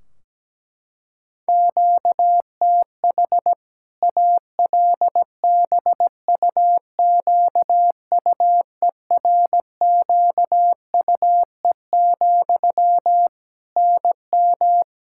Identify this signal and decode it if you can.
Morse code, heard as a keyed tone, about 17 wpm, 700 Hz. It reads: QTH ALBUQUERQUE, NM
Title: Practice QSO - Exchange 1 - Beginning - 17wpm